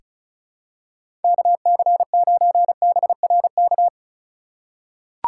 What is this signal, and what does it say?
Morse code, heard as a keyed tone, about 34 wpm, 700 Hz. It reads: KC9BRK
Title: USA Call Signs - 4x Character Spacing - 35wpm